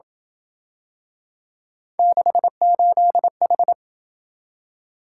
Morse code, heard as a keyed tone, about 27 wpm, 700 Hz. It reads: 68H